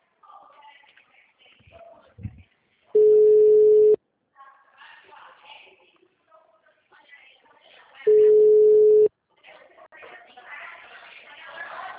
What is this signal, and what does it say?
Ringback tone (Europe, 425 Hz), with speech.